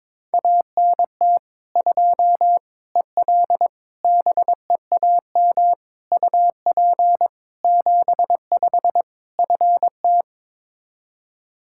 Morse code, heard as a keyed tone, about 22 wpm, 700 Hz. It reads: ANT 2 EL BEAM UP 75 FT
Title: K1USN SST QSO Practice 22wpm